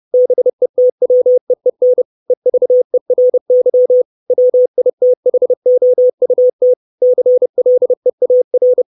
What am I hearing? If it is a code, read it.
Morse code, heard as a keyed tone, about 30 wpm, 500 Hz. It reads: BETWEEN EVERY WITHOUT CLEAR
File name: 632 Words 30WPM